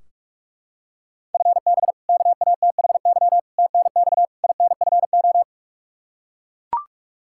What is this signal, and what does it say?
Morse code, heard as a keyed tone, about 42 wpm, 700 Hz. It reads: FB KATHY TNX INFO
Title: Practice QSO - Exchange 3 - More Info - 45wpm